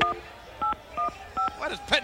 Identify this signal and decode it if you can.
DTMF, heard as telephone touch tones, tones of about 120 ms, with speech. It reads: 1815